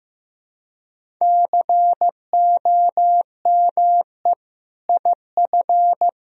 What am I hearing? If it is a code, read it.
Morse code, heard as a keyed tone, about 15 wpm, 700 Hz. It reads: COME IF
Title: Sentences from Top 1000 Words - Review Part 1 - 15wpm